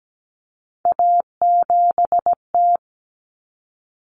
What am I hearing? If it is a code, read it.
Morse code, heard as a keyed tone, about 17 wpm, 700 Hz. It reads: A7T
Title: Common Cut Numbers - 3-Digit 17wpm